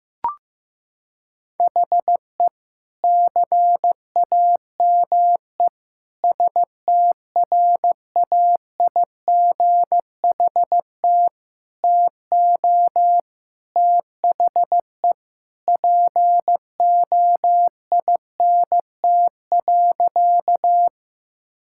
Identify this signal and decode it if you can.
Morse code, heard as a keyed tone, about 16 wpm, 700 Hz. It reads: HE CAME STRAIGHT TO THE POINT.
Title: Sentences from Top 1000 Words - Review Part 6 - 15wpm